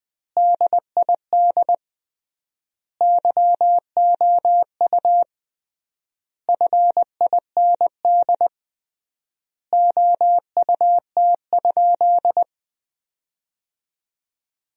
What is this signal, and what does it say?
Morse code, heard as a keyed tone, about 20 wpm, 700 Hz. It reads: DID YOU FIND OUT?
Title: 3x Word-Spacing - Sentences from Top 200 Words - 20wpm